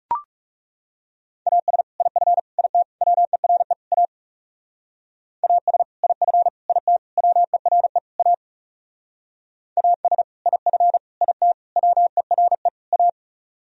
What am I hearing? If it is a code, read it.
Morse code, heard as a keyed tone, about 37 wpm, 700 Hz. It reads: AS IF IT WERE A AS IF IT WERE A AS IF IT WERE A
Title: Speed-Racing - Sets of 5 Words 30wpm